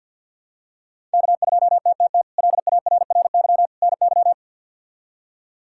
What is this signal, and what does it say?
Morse code, heard as a keyed tone, about 51 wpm, 700 Hz. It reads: K2TTT LARRY NY
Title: K1USN SST QSO Practice - 50wpm